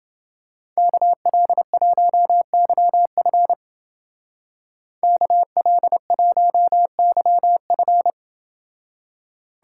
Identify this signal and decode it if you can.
Morse code, heard as a keyed tone, about 29 wpm, 700 Hz. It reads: KL1YF KL1YF